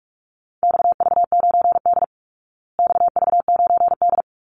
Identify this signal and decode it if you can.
Morse code, heard as a keyed tone, about 44 wpm, 700 Hz. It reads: XV9D XV9D
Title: Call Signs Repeated 3x - 45wpm